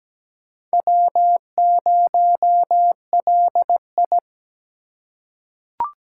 Morse code, heard as a keyed tone, about 17 wpm, 700 Hz. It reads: W0LI
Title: USA Call Signs - 3x Character Spacing - 17wpm